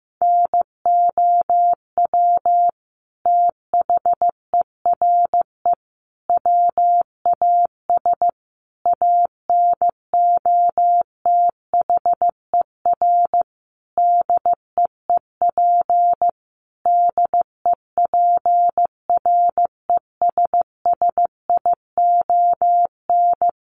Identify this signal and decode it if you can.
Morse code, heard as a keyed tone, about 15 wpm, 700 Hz. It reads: NOW THERE WAS ANOTHER DEEP DEPRESSION
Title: Sentences from Top 1300 Words 15wpm